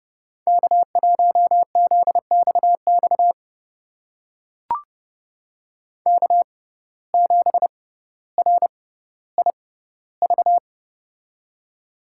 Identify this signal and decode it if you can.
Morse code, heard as a keyed tone, about 29 wpm, spaced as for 15 wpm, 700 Hz. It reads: K1ZXX K7RIV
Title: USA Call Signs - 5 Characters - 6x Character Spacing - 30wpm